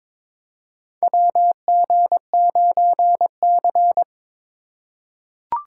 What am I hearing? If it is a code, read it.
Morse code, heard as a keyed tone, about 23 wpm, 700 Hz. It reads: WG9C
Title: USA Call Signs - 4 Characters - 4x Character Spacing - 22wpm